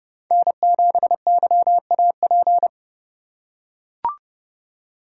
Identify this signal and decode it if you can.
Morse code, heard as a keyed tone, about 30 wpm, 700 Hz. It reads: N7YAP
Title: USA Call Signs - 5 Characters - 4x Character Spacing - 30wpm